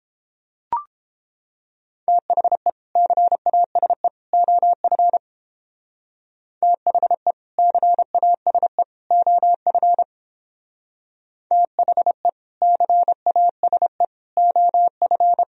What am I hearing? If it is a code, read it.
Morse code, heard as a keyed tone, about 29 wpm, 700 Hz. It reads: THE CASE OF THE CASE OF THE CASE OF